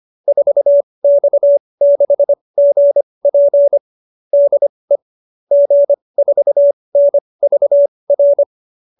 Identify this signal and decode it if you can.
Morse code, heard as a keyed tone, about 25 wpm, spaced as for 20 wpm, 570 Hz. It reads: 4X6GP DE G4NVR